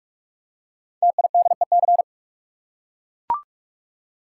Morse code, heard as a keyed tone, about 44 wpm, 700 Hz. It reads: TIDEC
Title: Code Groups - 5 Letters - 45wpm